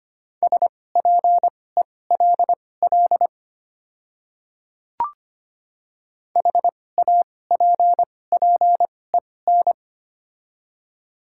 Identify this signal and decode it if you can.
Morse code, heard as a keyed tone, about 25 wpm, spaced as for 18 wpm, 700 Hz. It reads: SPELL HAPPEN
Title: 2x Character Spacing - Top 500 Words - 25wpm